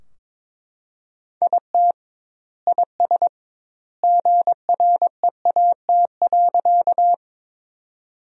Morse code, heard as a keyed tone, about 22 wpm, 700 Hz. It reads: IT IS GREAT.
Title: 2x Word-Spacing - Sentences from Top 200 Words - 22wpm